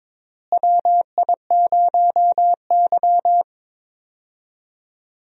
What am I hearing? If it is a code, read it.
Morse code, heard as a keyed tone, about 21 wpm, 700 Hz. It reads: WI0Y
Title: USA Call Signs - 4 Characters - Repeated 2x - 22wpm